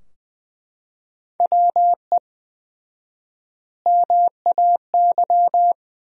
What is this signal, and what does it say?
Morse code, heard as a keyed tone, about 20 wpm, 700 Hz. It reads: WE MAY
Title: 4x Word-Spacing - Sentences from Top 300 Words - 20wpm